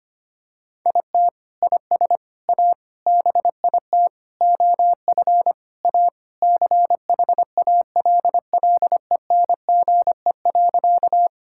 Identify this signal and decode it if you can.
Morse code, heard as a keyed tone, about 25 wpm, 700 Hz. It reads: IT IS A BIT OF A CHALLENGE.
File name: practice / 25wpm / Sentences with 4-Word Phrases - 25wpm